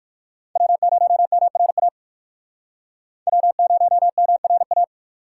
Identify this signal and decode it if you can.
Morse code, heard as a keyed tone, about 64 wpm, spaced as for 56 wpm, 700 Hz. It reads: W0MRA W0MRA